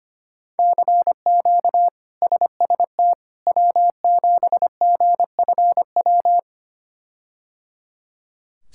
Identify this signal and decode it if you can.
Morse code, heard as a keyed tone, about 25 wpm, 700 Hz. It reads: CQ SST W7GFW
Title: K1USN SST QSO Practice - 25wpm